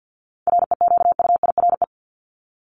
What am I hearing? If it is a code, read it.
Morse code, heard as a keyed tone, about 50 wpm, 700 Hz. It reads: REQUIRE
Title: Top 751-1000 Words 50wpm